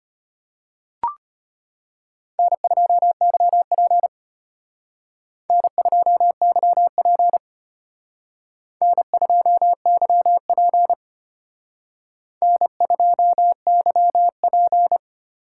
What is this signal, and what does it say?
Morse code, heard as a keyed tone, about 32 wpm, 700 Hz. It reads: N2YP N2YP N2YP N2YP